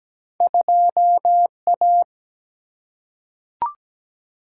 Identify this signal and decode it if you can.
Morse code, heard as a keyed tone, about 17 wpm, 700 Hz. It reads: 2A
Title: Single-Letter Single-Number 17wpm